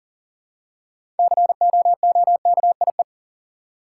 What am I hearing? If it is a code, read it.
Morse code, heard as a keyed tone, about 38 wpm, 700 Hz. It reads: COOKIE
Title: Six Letter Words - 40wpm